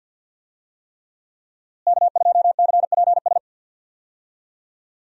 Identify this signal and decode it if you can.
Morse code, heard as a keyed tone, about 51 wpm, 700 Hz. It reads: K2CPS
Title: USA Call Signs - 5 Characters - Repeated 3x - 50wpm